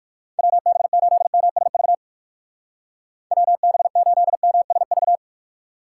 Morse code, heard as a keyed tone, about 61 wpm, 700 Hz. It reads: WB8MSV WB8MSV